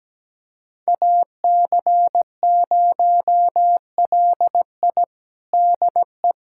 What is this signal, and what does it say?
Morse code, heard as a keyed tone, about 17 wpm, 700 Hz. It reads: AC0LI DE
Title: USA Callsign DE Callsign 17wpm